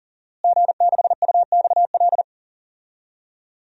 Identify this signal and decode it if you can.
Morse code, heard as a keyed tone, about 40 wpm, 700 Hz. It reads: G6UXL
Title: DXCC-100 Call Signs - 5 Characters - Repeated 3x - 40wpm